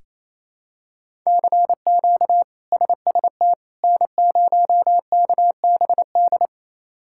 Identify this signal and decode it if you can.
Morse code, heard as a keyed tone, about 28 wpm, 700 Hz. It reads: CQ SST N0KBD
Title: K1USN SST QSO Practice - 28wpm